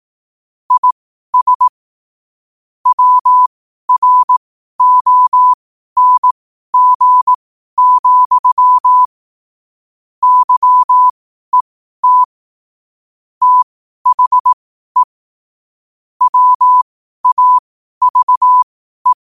Morse code, heard as a keyed tone, about 18 wpm, spaced as for 12 wpm, 1000 Hz. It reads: IS WRONG, YET THE WAVE
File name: qotdfiles / 12wpm